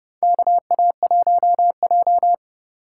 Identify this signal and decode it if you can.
Morse code, heard as a keyed tone, about 30 wpm, 700 Hz. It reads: KA1J
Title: USA Call Signs - 4 Characters - 3x Character Spacing - 30wpm